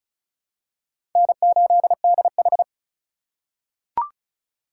Morse code, heard as a keyed tone, about 34 wpm, 700 Hz. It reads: N8DH
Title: USA Call Signs - 3x Character Spacing - 35wpm